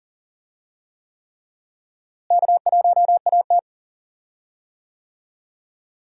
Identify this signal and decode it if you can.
Morse code, heard as a keyed tone, about 40 wpm, 700 Hz. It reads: K1AT
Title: USA Call Signs - 4 Characters - Repeated 3x - 40wpm